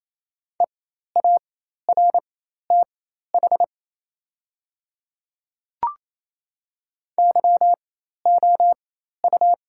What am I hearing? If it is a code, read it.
Morse code, heard as a keyed tone, about 28 wpm, spaced as for 13 wpm, 700 Hz. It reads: EARTH YOU
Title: 4x Character Spacing - Top 200 Words - 28wpm